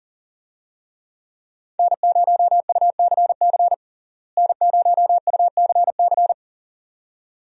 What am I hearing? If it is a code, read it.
Morse code, heard as a keyed tone, about 40 wpm, 700 Hz. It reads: N0UCC N0UCC
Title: USA Call Signs - 5 Characters - Repeated 2x - 40wpm